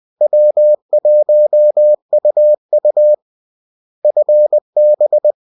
Morse code, heard as a keyed tone, about 20 wpm, 600 Hz. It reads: W1UU FB